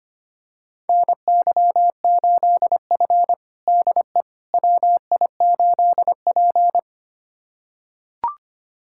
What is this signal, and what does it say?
Morse code, heard as a keyed tone, about 25 wpm, 700 Hz. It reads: NY8F DE WI8P
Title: USA Callsign DE Callsign - 4 Characters - 25wpm